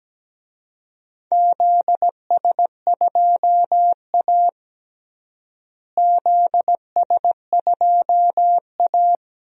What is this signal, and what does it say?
Morse code, heard as a keyed tone, about 17 wpm, 700 Hz. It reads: ZS2A ZS2A
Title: DXCC-100 Call Signs - 4 Characters - Repeated 2x - 17wpm